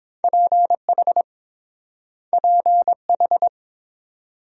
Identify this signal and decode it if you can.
Morse code, heard as a keyed tone, about 30 wpm, 700 Hz. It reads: PH PH